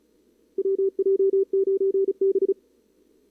Morse code, heard as a keyed tone, about 35 wpm, 410 Hz. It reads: WJ9B
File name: CWSprint / Feb2022 / k5tr-feb-na-sprint-cw-k5ot -0000